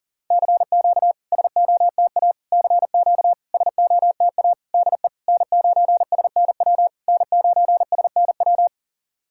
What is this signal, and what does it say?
Morse code, heard as a keyed tone, about 43 wpm, 700 Hz. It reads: CQ SOTA CQ SOTA DE N9SNW N9SNW
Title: SOTA QSO Practice 40wpm